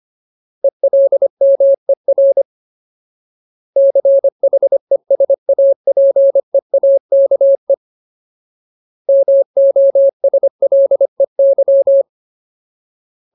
Morse code, heard as a keyed tone, about 25 wpm, 550 Hz. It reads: ELMER CHESAPEAKE MOSLEY
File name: Ultimate Word List / 25WPM Ultimate